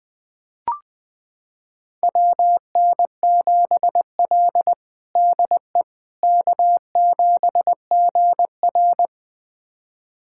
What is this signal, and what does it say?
Morse code, heard as a keyed tone, about 21 wpm, 700 Hz. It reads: WN7L DE K7GR
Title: USA Callsign DE Callsign - 4 Characters - 20wpm